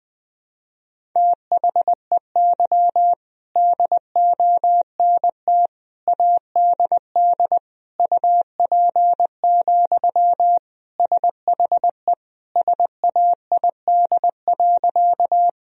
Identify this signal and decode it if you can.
Morse code, heard as a keyed tone, about 20 wpm, 700 Hz. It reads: THEY DONT ADD UP, SHE SAID.
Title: Sentences from Top 1000 Words - Review Part 1 - 20wpm